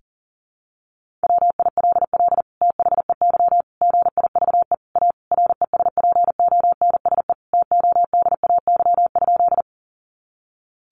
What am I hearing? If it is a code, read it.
Morse code, heard as a keyed tone, about 40 wpm, 700 Hz. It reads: WILL THEY GIVE A RESPONSE TODAY?
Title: Sentences from Top 1400 Words 40wpm